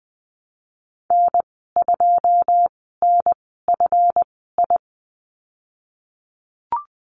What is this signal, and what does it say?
Morse code, heard as a keyed tone, about 20 wpm, spaced as for 15 wpm, 700 Hz. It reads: N2NFI
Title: 245 Lesson 39g - Letter z - Call Signs 20wpm